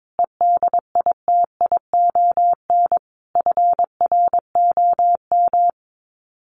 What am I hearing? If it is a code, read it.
Morse code, heard as a keyed tone, about 22 wpm, 700 Hz. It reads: EDITION FROM
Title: Sets of 5 Words Encore2 22wpm